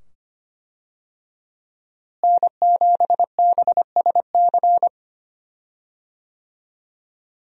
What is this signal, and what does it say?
Morse code, heard as a keyed tone, about 25 wpm, 700 Hz. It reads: N7BSC
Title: USA Call Signs - 5 Characters - Repeated 2x - 25wpm